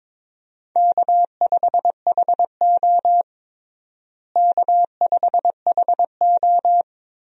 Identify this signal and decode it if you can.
Morse code, heard as a keyed tone, about 22 wpm, 700 Hz. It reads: K5HO K5HO
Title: USA Call Signs - 4 Characters - Repeated 3x - 22wpm